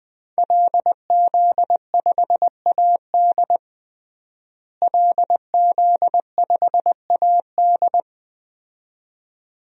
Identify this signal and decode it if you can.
Morse code, heard as a keyed tone, about 20 wpm, 700 Hz. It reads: LZ5AD LZ5AD
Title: DXCC-100 Call Signs - 5 Characters - Repeated 2x - 20wpm